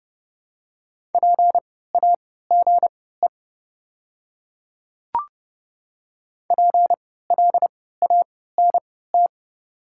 Morse code, heard as a keyed tone, about 31 wpm, spaced as for 17 wpm, 700 Hz. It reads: PAGE PLANT
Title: 3x Character Spacing - Top 500 Words - 30wpm